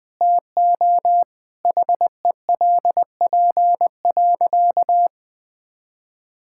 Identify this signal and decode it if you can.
Morse code, heard as a keyed tone, about 20 wpm, 700 Hz. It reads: TO HELP.
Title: Sentences with 8-Word Phrases - 20wpm